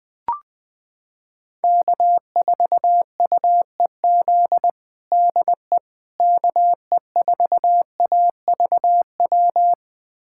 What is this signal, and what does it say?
Morse code, heard as a keyed tone, about 21 wpm, 700 Hz. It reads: K4UEZ DE KE4AVW
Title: USA Callsign DE Callsign 20wpm